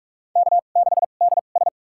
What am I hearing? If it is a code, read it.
Morse code, heard as a keyed tone, about 44 wpm, spaced as for 32 wpm, 700 Hz. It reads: K6DS
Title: USA Call Signs - 2x Character Spacing - 45wpm